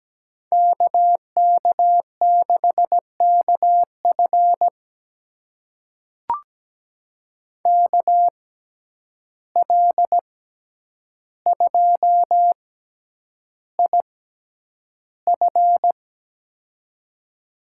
Morse code, heard as a keyed tone, about 17 wpm, spaced as for 9 wpm, 700 Hz. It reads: KK6KF KL2IF
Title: USA Call Signs - 5 Characters - 6x Character Spacing - 17wpm